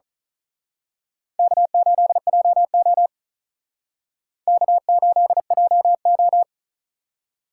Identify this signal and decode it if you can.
Morse code, heard as a keyed tone, about 39 wpm, 700 Hz. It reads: K8JO K8JO
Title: Speed Racing - USA 1x2 Call Signs 35wpm